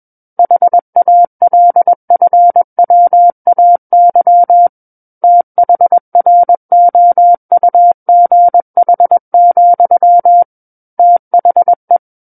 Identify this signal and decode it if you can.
Morse code, heard as a keyed tone, about 21 wpm, 700 Hz. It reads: HALFWAY THROUGH, THE
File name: Book One / 12-Chapter 12 - 21 WPM